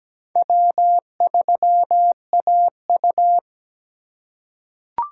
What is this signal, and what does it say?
Morse code, heard as a keyed tone, about 17 wpm, 700 Hz. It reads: W3AU
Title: USA Call Signs - 3x Character Spacing - 17wpm